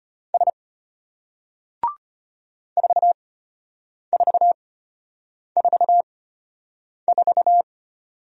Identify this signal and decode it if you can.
Morse code, heard as a keyed tone, about 33 wpm, spaced as for 8 wpm, 700 Hz. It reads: S 4444